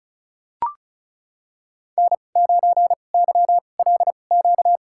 Morse code, heard as a keyed tone, about 36 wpm, spaced as for 25 wpm, 700 Hz. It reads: N9YLQ
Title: USA Call Signs - 5 Characters - 2x Character Spacing - 35wpm